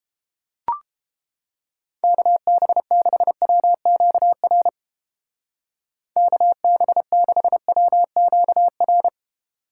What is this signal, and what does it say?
Morse code, heard as a keyed tone, about 29 wpm, 700 Hz. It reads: KB6WQR KB6WQR